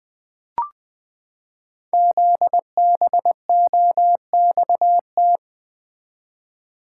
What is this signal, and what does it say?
Morse code, heard as a keyed tone, about 20 wpm, 700 Hz. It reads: ZBOXT
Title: Code Groups - 5 Letters - 20wpm